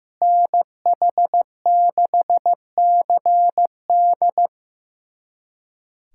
Morse code, heard as a keyed tone, about 15 wpm, 700 Hz. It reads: NH6CD
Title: USA Callsign DE Callsign - 5 Characters - 15wpm